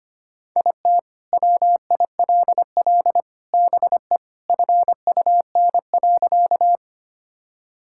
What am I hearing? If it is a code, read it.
Morse code, heard as a keyed tone, about 25 wpm, 700 Hz. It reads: IT WILL BE FUN.
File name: practice / 25wpm / Sentences with 3-Word Phrases - 25wpm